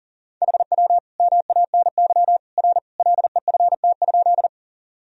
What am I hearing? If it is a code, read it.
Morse code, heard as a keyed tone, about 40 wpm, 700 Hz. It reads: HW MANY R LEFT?
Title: Sentences from Top 300 Words and Abbreviations 40wpm